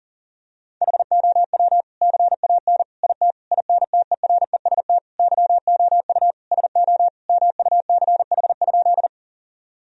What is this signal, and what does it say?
Morse code, heard as a keyed tone, about 40 wpm, 700 Hz. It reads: HOW CAN IT INTEREST YOU SO MUCH?
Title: Sentences from Top 1000 Words - Review Part 2 - 40wpm